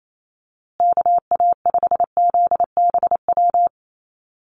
Morse code, heard as a keyed tone, about 28 wpm, 700 Hz. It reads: KA5ZBW
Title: USA Call Signs Repeated 2x 28wpm